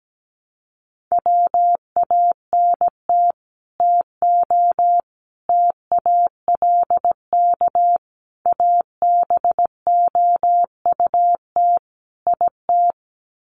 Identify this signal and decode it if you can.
Morse code, heard as a keyed tone, about 17 wpm, 700 Hz. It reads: WANT TO TALK ABOUT IT
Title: Sets of 5 Words Encore 17wpm